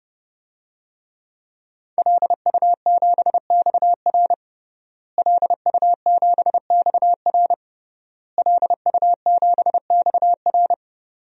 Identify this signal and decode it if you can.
Morse code, heard as a keyed tone, about 30 wpm, 700 Hz. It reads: LU7XR LU7XR LU7XR
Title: DXCC-100 Call Signs - 5 Characters - Repeated 3x - 30wpm